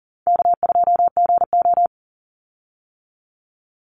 Morse code, heard as a keyed tone, about 40 wpm, 700 Hz. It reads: K2GO
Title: Easy Call Signs - 40wpm